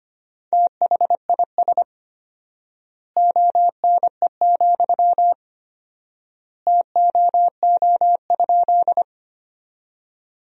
Morse code, heard as a keyed tone, about 25 wpm, 700 Hz. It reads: THIS ONE, TOO?
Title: 4x Word-Spacing - Sentences from Top 200 Words - 25wpm